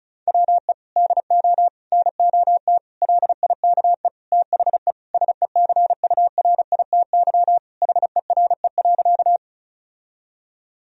Morse code, heard as a keyed tone, about 36 wpm, 700 Hz. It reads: WE DO NOT LIKE THE SECURITY HERE.
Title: Sentences from Top 1000 Words - Review Part 4 - 35wpm